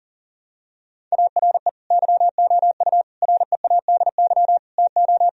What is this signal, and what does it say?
Morse code, heard as a keyed tone, about 40 wpm, 700 Hz. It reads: ARE YOU READY TO
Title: Sentences from Top 1000 Words - Review Part 1 - 40wpm